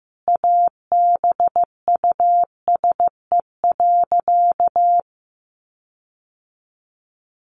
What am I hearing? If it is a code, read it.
Morse code, heard as a keyed tone, about 15 wpm, 700 Hz. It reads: ABUSE.
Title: Sentences from Top 1300 Words 15wpm